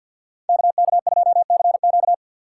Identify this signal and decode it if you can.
Morse code, heard as a keyed tone, about 53 wpm, 700 Hz. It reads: KK2CQ
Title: USA Call Signs - 5 Characters - 4x Character Spacing - 50wpm